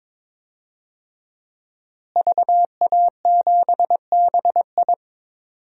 Morse code, heard as a keyed tone, about 21 wpm, 700 Hz. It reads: VA7BI
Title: DXCC-100 Call Signs - 5 Characters - Repeated 2x - 22wpm